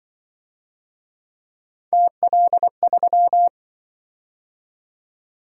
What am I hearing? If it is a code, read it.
Morse code, heard as a keyed tone, about 24 wpm, 700 Hz. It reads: TL3
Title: Speed-Racing - Three Letter-Number Combinations - 20wpm